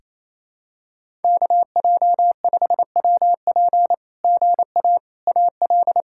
Morse code, heard as a keyed tone, about 27 wpm, 700 Hz. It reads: KJ5WP GA AL
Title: SOTA QSO Practice 28wpm